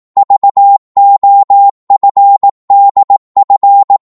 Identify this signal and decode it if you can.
Morse code, heard as a keyed tone, about 18 wpm, 800 Hz. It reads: VOFDF
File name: Morse / ltr-18wpm